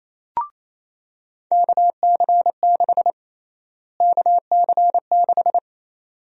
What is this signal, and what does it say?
Morse code, heard as a keyed tone, about 28 wpm, 700 Hz. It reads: KC6 KC6
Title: USA Call Sign Prefixes - Repeated 2x - 28wpm